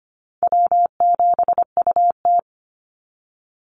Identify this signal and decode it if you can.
Morse code, heard as a keyed tone, about 25 wpm, 700 Hz. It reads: W7UT
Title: Easy Call Signs - 25wpm